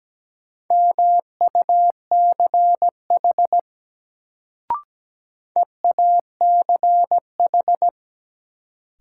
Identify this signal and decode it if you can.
Morse code, heard as a keyed tone, about 17 wpm, 700 Hz. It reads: MUCH EACH
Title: Sentences from Top 200 Words - New Word Drill - 17wpm